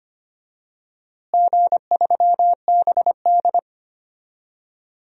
Morse code, heard as a keyed tone, about 25 wpm, 700 Hz. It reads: G3BD
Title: DXCC-100 Call Signs - 4 Characters - Repeated 2x - 25wpm